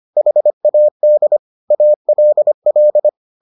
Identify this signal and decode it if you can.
Morse code, heard as a keyed tone, about 25 wpm, 600 Hz. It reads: HAD ALL